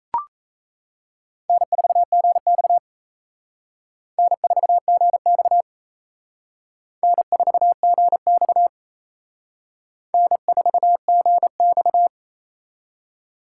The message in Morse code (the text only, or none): N4GX N4GX N4GX N4GX